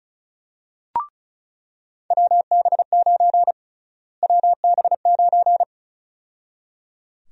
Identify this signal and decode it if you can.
Morse code, heard as a keyed tone, about 34 wpm, 700 Hz. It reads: WB9 WB9
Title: USA Call Sign Prefixes - Repeated 2x - 35wpm